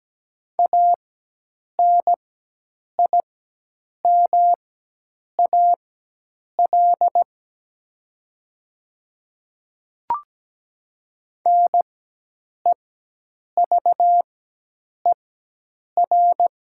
Morse code, heard as a keyed tone, about 17 wpm, spaced as for 8 wpm, 700 Hz. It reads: ANIMAL NEVER
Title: 4x Character Spacing - Top 300 Words - 17wpm